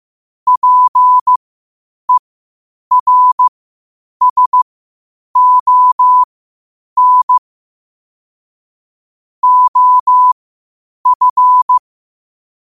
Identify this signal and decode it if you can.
Morse code, heard as a keyed tone, about 15 wpm, spaced as for 8 wpm, 1000 Hz. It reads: PERSON OF
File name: qotdfiles / 8wpm